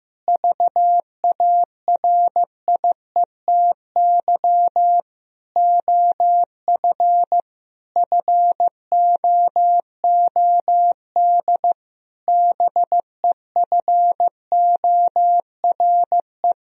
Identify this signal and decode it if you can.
Morse code, heard as a keyed tone, about 15 wpm, 700 Hz. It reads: VARIETY OF FOOD BEFORE
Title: Sentences from Top 1000 Words - Review Part 4 - 15wpm